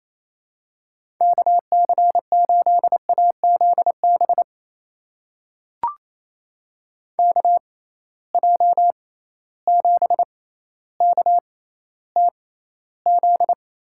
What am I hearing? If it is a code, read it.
Morse code, heard as a keyed tone, about 27 wpm, spaced as for 14 wpm, 700 Hz. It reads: KC8AZB KJ7KTZ
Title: USA Call Signs - 6x Character Spacing - 28wpm